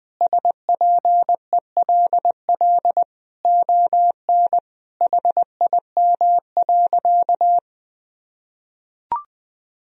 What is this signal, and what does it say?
Morse code, heard as a keyed tone, about 20 wpm, 700 Hz. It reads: SPELL ON HIM.
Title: Sentences from Top 1000 Words - Review Part 1 - 20wpm